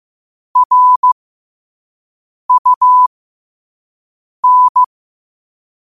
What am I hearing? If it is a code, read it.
Morse code, heard as a keyed tone, about 15 wpm, spaced as for 5 wpm, 1000 Hz. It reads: RUN
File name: qotdfiles / 5wpm